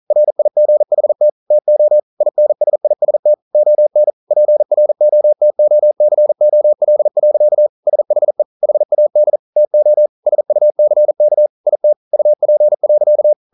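Morse code, heard as a keyed tone, about 43 wpm, 600 Hz. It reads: RIGHT TO INSIST ON PROTOCOL. SHE HAD TO SUCK IT UP.
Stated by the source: Morse code practice at 41 wpm, from CW med Gnister.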